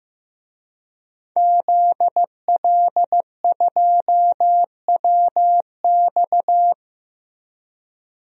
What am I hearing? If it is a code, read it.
Morse code, heard as a keyed tone, about 15 wpm, 700 Hz. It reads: ZL2WX
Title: DXCC-100 Call Signs - 5 Characters - Repeated 2x - 15wpm